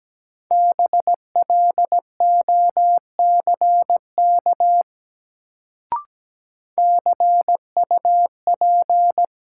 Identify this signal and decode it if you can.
Morse code, heard as a keyed tone, about 17 wpm, 700 Hz. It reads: BLOCK CUP
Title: Sentences from Top 1600 Words - New Word Drill - 17wpm